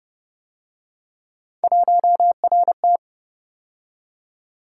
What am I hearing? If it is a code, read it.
Morse code, heard as a keyed tone, about 28 wpm, 700 Hz. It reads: 1RT